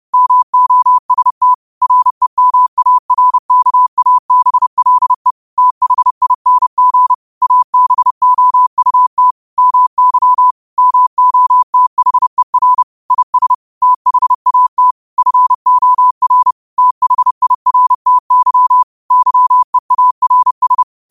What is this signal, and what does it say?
Morse code, heard as a keyed tone, about 30 wpm, 1000 Hz. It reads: MOST REMARKABLE THING ABOUT MY MOTHER IS THAT FOR THIRTY YEARS
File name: qotdfiles / 30wpm